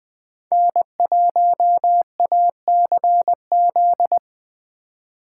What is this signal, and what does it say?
Morse code, heard as a keyed tone, about 20 wpm, 700 Hz. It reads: N1ACZ
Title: Speed Racing - USA 1x3 Call Signs 17wpm